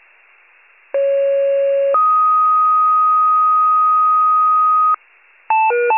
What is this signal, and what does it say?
Two-tone page, heard as a one-second tone, then a three-second tone, about 550 then 1250 Hz. It is heard from 0.9 to 5.0 s.
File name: wcalert